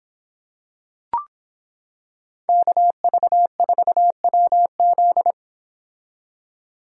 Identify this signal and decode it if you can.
Morse code, heard as a keyed tone, about 26 wpm, 700 Hz. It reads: KV4WZ